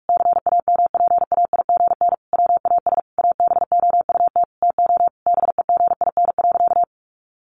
Morse code, heard as a keyed tone, about 44 wpm, 700 Hz. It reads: CAMPAIGN WAS ABOUT TO BEGIN.
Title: Sentences from Top 1200 Words 45wpm